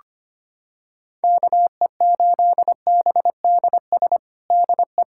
Morse code, heard as a keyed tone, about 25 wpm, 700 Hz. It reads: KE8BDS DE
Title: USA Callsign DE Callsign 25wpm